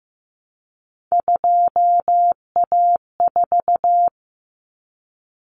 Morse code, heard as a keyed tone, about 15 wpm, 700 Hz. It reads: 2A4
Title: Common Cut Numbers - 3-Digit 15wpm